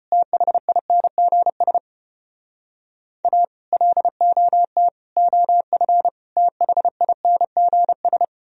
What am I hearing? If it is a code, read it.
Morse code, heard as a keyed tone, about 31 wpm, 700 Hz. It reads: THINGS A LOT OF THINGS